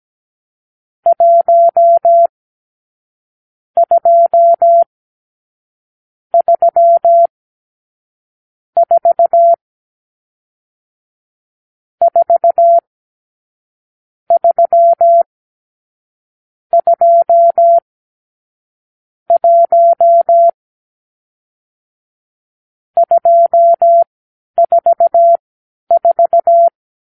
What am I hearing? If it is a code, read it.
Morse code, heard as a keyed tone, about 17 wpm, spaced as for 11 wpm, 680 Hz. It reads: 1 2 3 4 4 3 2 1 244
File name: k7qo / Lesson 46 (1-4)